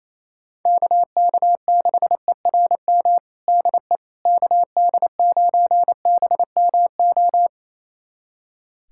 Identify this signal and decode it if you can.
Morse code, heard as a keyed tone, about 28 wpm, 700 Hz. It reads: KK6ERM DE KD9BMO
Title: USA Callsign DE Callsign - 6 Characters - 28wpm